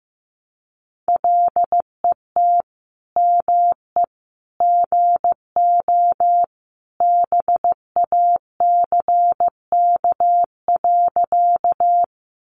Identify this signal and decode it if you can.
Morse code, heard as a keyed tone, about 15 wpm, 700 Hz. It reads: LET ME GO BACK.
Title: Sentences from Top 300 Words 15wpm